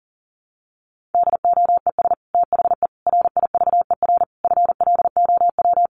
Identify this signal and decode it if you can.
Morse code, heard as a keyed tone, about 40 wpm, 700 Hz. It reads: DOES THE RIVER FLOW
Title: Sentences from Top 1200 Words 40wpm